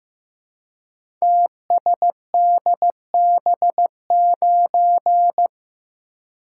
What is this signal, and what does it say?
Morse code, heard as a keyed tone, about 15 wpm, 700 Hz. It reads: TSDB9
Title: Code Groups - 5 Letter:Numbers - 15wpm